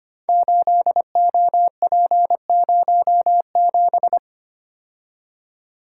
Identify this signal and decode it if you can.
Morse code, heard as a keyed tone, about 25 wpm, 700 Hz. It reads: 8OP07
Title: Code Groups - 5 Letter:Numbers - 25wpm